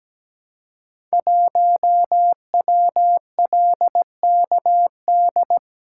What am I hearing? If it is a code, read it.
Morse code, heard as a keyed tone, about 17 wpm, 700 Hz. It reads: 1WLKD
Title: Code Groups - 5 Letter:Numbers - 17wpm